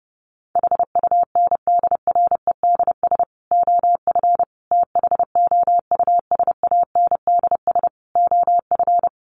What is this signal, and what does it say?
Morse code, heard as a keyed tone, about 30 wpm, 700 Hz. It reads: HUNDREDS OF THOUSANDS OF
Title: Sets of 5 Words Encore 30wpm